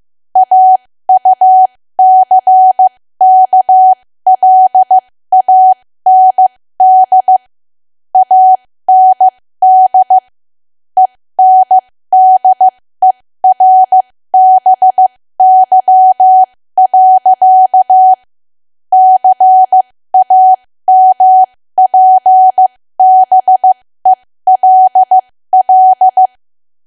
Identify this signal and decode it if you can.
Morse code, heard as a keyed tone, about 15 wpm, spaced as for 13 wpm, 750 Hz. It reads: AUCKLAND AND ENDERBY. CAMPBELL
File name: cwa / W1AW-Level2 / 13WPM